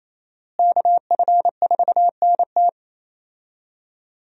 Morse code, heard as a keyed tone, about 28 wpm, 700 Hz. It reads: KF4NT
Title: USA Call Signs - 5 Characters - Repeated 3x - 28wpm